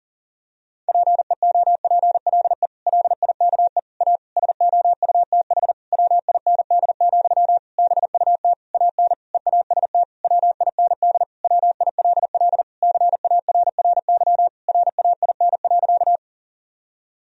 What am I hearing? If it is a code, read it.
Morse code, heard as a keyed tone, about 40 wpm, 700 Hz. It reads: PEOPLE LIKE A SOUTH WIND, BUT AN EAST WIND WILL CARRY RAIN.
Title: Sentences from Top 1000 Words - Review Part 3 - 40wpm